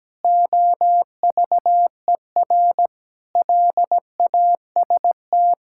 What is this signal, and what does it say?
Morse code, heard as a keyed tone, about 17 wpm, 700 Hz. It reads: OVER LAST
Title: Sentences from Top 1000 Words - Review Part 5 - 17wpm